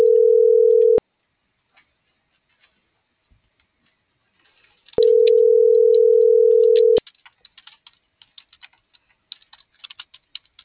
Ringback tone (North America), 440 + 480 Hz.